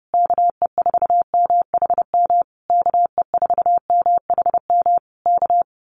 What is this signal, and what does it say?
Morse code, heard as a keyed tone, about 30 wpm, 700 Hz. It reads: KE4MHM KE4MHM K
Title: Common QSO Phrases Encore 30wpm